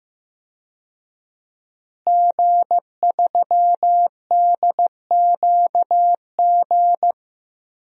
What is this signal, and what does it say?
Morse code, heard as a keyed tone, about 15 wpm, 700 Hz. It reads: G3DQG
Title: DXCC-100 Call Signs - 5 Characters - Repeated 3x - 15wpm